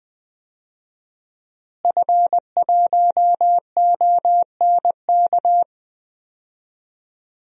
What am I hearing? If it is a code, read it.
Morse code, heard as a keyed tone, about 21 wpm, 700 Hz. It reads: F1ONK
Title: DXCC-100 Call Signs - 5 Characters - Repeated 2x - 20wpm